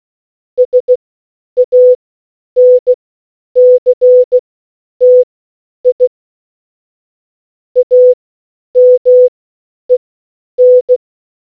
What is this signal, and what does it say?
Morse code, heard as a keyed tone, about 16 wpm, spaced as for 9 wpm, 500 Hz. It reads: SANCTI AMEN